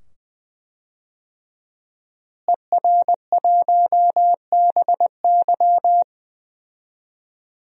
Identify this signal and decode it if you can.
Morse code, heard as a keyed tone, about 20 wpm, 700 Hz. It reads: ER1BY
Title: DXCC-100 Call Signs - 5 Characters - Repeated 2x - 20wpm